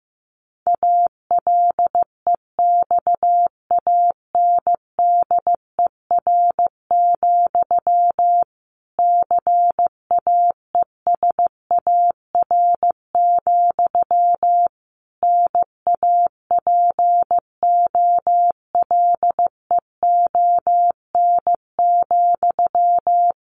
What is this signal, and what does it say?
Morse code, heard as a keyed tone, about 15 wpm, 700 Hz. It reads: ALEXANDER, CAESAR, NAPOLEON,
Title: A Sound Of Thunder 15wpm